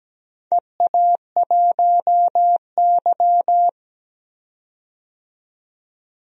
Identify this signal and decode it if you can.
Morse code, heard as a keyed tone, about 17 wpm, 700 Hz. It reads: EA1Y
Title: DXCC-100 Call Signs - 4 Characters - 17wpm